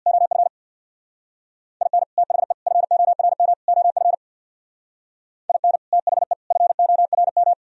Morse code, heard as a keyed tone, about 64 wpm, 700 Hz. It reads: OF IN THE FORM OF IN THE FORM